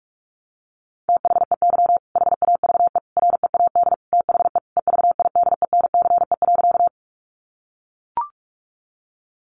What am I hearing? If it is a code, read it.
Morse code, heard as a keyed tone, about 44 wpm, 700 Hz. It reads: THEY HAVE READ THE EVIDENCE.
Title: Sentences from Top 1300 Words 45wpm